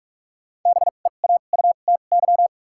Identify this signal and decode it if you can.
Morse code, heard as a keyed tone, about 46 wpm, spaced as for 32 wpm, 700 Hz. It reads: BEAUTY
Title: 2x Character Spacing - Top 500 Words - 45wpm